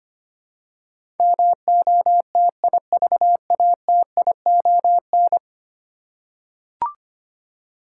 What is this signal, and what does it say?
Morse code, heard as a keyed tone, about 25 wpm, 700 Hz. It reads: MOTIVATION
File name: Ten Letter Words - 25wpm